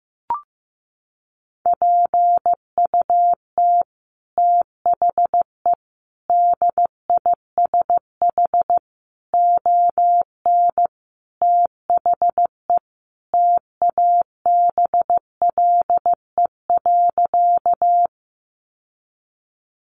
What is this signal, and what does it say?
Morse code, heard as a keyed tone, about 15 wpm, 700 Hz. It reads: PUT THE DISH ON THE TABLE.
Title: Sentences from Top 1300 Words 15wpm